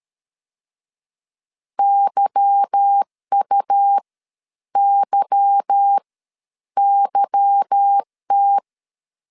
Morse code, heard as a keyed tone, about 13 wpm, spaced as for 9 wpm, 800 Hz. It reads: YUYYT
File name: LEZIONI / 8 YJ45 / 8 Y 12wpm 2228